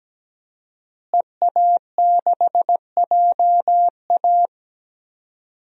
Morse code, heard as a keyed tone, about 17 wpm, 700 Hz. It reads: EA6JA
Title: DXCC-100 Call Signs - 5 Characters - Repeated 2x - 17wpm